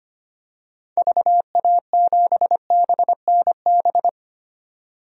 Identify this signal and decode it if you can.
Morse code, heard as a keyed tone, about 25 wpm, 700 Hz. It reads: VA7BNB
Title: DXCC-100 Call Signs - 6 Characters - 25wpm